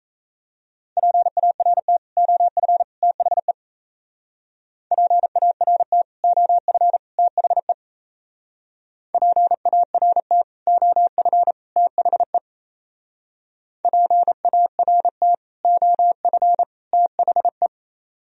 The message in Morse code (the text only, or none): PART OF THE PART OF THE PART OF THE PART OF THE